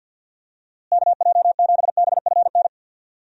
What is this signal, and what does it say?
Morse code, heard as a keyed tone, about 51 wpm, 700 Hz. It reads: KJ7BFN
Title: USA Call Signs - 6 Characters - Repeated 3x - 50wpm